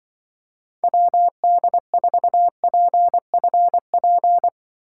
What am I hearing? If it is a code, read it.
Morse code, heard as a keyed tone, about 24 wpm, 700 Hz. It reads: WD4PFP
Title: Speed Racing - USA 2x3 Call Signs 20wpm